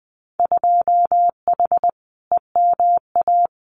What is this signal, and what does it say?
Morse code, heard as a keyed tone, about 20 wpm, 700 Hz. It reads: 2H EMA
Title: Winter Field Day Exchanges 20wpm